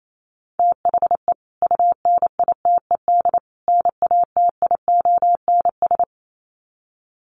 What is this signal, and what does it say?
Morse code, heard as a keyed tone, about 28 wpm, 700 Hz. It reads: THE UNITED NATIONS
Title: Sets of 3 Words 28wpm